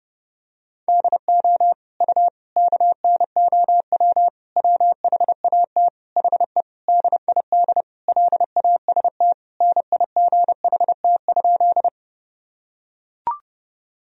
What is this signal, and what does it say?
Morse code, heard as a keyed tone, about 29 wpm, 700 Hz. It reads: DO U KNOW WHAT HE DID LAST NIGHT?
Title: Sentences from Top 300 Words and Abbreviations 30wpm